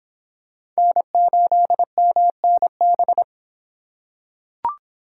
Morse code, heard as a keyed tone, about 26 wpm, 700 Hz. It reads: N8MNB